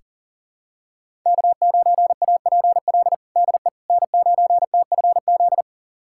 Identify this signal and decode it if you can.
Morse code, heard as a keyed tone, about 40 wpm, 700 Hz. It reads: K9APL DE N9TFZ
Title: USA Callsign DE Callsign - 5 Characters - 40wpm